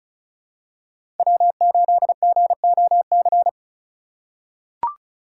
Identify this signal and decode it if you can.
Morse code, heard as a keyed tone, about 36 wpm, 700 Hz. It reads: W8GOC
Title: USA Call Signs - 5 Characters - 6x Character Spacing - 35wpm